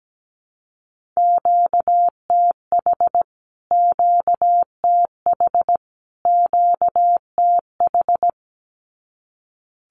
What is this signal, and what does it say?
Morse code, heard as a keyed tone, about 17 wpm, 700 Hz. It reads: QTH QTH QTH
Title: Q-Codes Repeated 3x 17wpm